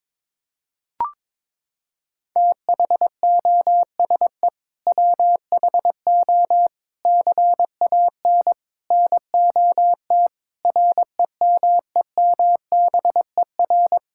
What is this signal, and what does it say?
Morse code, heard as a keyed tone, about 22 wpm, 700 Hz. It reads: THOSE WHO CAN NOT REMEMBER
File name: practice / 22wpm / Sets of 8 Words - 22wpm